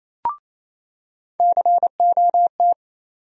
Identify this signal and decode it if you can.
Morse code, heard as a keyed tone, about 28 wpm, 700 Hz. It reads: COT
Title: USA Call Sign Postfixes - 3 Letters 28wpm